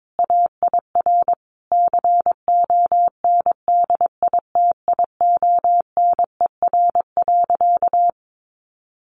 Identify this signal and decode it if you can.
Morse code, heard as a keyed tone, about 22 wpm, 700 Hz. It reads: AIR CONDITIONER.
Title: Easy Phrases 22wpm